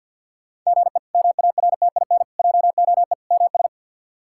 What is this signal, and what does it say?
Morse code, heard as a keyed tone, about 48 wpm, 700 Hz. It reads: GE MARTIN JOE MS
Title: K1USN SST QSO Practice - 50wpm